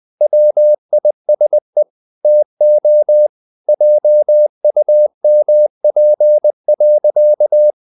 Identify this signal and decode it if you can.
Morse code, heard as a keyed tone, about 20 wpm, 600 Hz. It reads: WISE TO JUMP.
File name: the old man and the sea / the old man and the sea part 1 20wpm